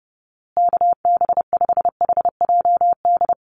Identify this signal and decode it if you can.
Morse code, heard as a keyed tone, about 30 wpm, 700 Hz. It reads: KB5HJD
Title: USA Call Signs Repeated 2x 30wpm